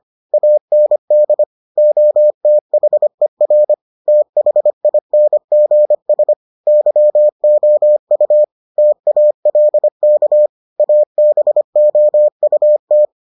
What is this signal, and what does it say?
Morse code, heard as a keyed tone, about 25 wpm, 600 Hz. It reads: AND OTHER THINGS YOU TALK ABOUT